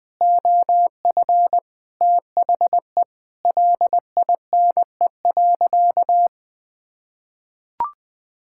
Morse code, heard as a keyed tone, about 20 wpm, 700 Hz. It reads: OF THE LINE.
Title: Sentences with 3-Word Phrases - Spoken Beginning - 20wpm